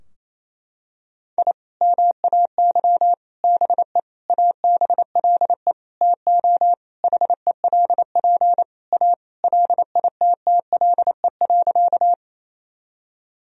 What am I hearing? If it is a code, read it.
Morse code, heard as a keyed tone, about 28 wpm, 700 Hz. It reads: I MAY BE ABLE TO HELP A LITTLE.
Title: Sentences from Top 1000 Words - Review Part 2 - 28wpm